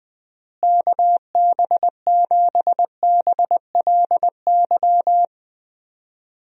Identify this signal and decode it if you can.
Morse code, heard as a keyed tone, about 21 wpm, 700 Hz. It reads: KB7BLY
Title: USA Call Signs - 6 Characters - Repeated 3x - 20wpm